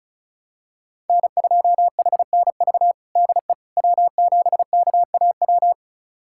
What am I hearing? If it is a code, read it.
Morse code, heard as a keyed tone, about 34 wpm, 700 Hz. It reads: N2HNV DE W7KAW
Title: USA Callsign DE Callsign - 5 Characters - 35wpm